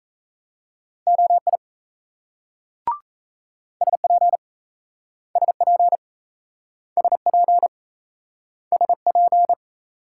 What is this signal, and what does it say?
Morse code, heard as a keyed tone, about 35 wpm, 700 Hz. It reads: OI SP SP SP SP
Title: Speed-Racing Multi-letter Phonemes - 28wpm